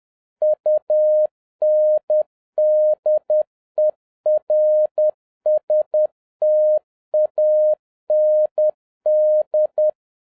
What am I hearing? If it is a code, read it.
Morse code, heard as a keyed tone, about 10 wpm, 610 Hz. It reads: UNDERSTAND
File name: 2011-07-mp3 / n41 EXCITING QRP CW OPERATION 10wpm